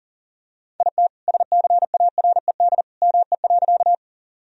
Morse code, heard as a keyed tone, about 40 wpm, 700 Hz. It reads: IT SCARED ME.
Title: Sentences from Top 1000 Words - Review Part 4 - 40wpm